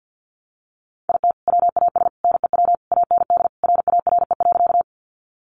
Morse code, heard as a keyed tone, about 50 wpm, 700 Hz. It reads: IT WAS NEW AND RARE.
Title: Sentences from Top 1100 Words 50wpm